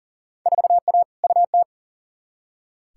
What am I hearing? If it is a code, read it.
Morse code, heard as a keyed tone, about 40 wpm, 700 Hz. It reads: 4A UT
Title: Field Day Exchanges - 40wpm